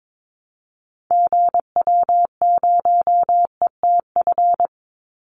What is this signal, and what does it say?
Morse code, heard as a keyed tone, about 22 wpm, 700 Hz. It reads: GW0ETF
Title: Call Signs Repeated 3x - 22wpm